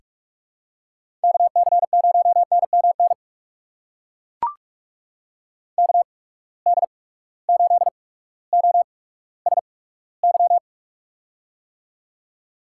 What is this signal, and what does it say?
Morse code, heard as a keyed tone, about 41 wpm, spaced as for 19 wpm, 700 Hz. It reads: KC0NMN KD8OSY
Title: USA Call Signs - 6 Characters - 8x Character Spacing - 45wpm